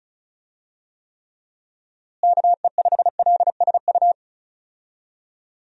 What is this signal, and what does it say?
Morse code, heard as a keyed tone, about 36 wpm, 700 Hz. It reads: KE5LSU
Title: USA Call Signs - 6 Characters - Repeated 2x - 35wpm